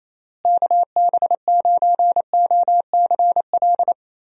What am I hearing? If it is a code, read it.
Morse code, heard as a keyed tone, about 28 wpm, 700 Hz. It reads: KB9OCL
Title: USA Call Signs - 6 Characters - Repeated 3x - 28wpm